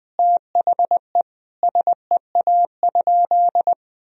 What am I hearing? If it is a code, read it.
Morse code, heard as a keyed tone, about 21 wpm, 700 Hz. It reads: THE SEA?
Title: Sentences from Top 300 Words and Abbreviations 20wpm